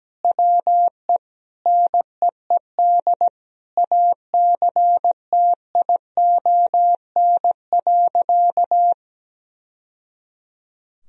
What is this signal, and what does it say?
Morse code, heard as a keyed tone, about 17 wpm, 700 Hz. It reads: WE NEED ACTION.
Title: Sentences from Top 1000 Words - Review Part 4 - 17wpm